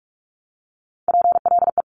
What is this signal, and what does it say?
Morse code, heard as a keyed tone, about 46 wpm, 700 Hz. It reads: PLE
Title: Top Three Letter Combinations in Words 45wpm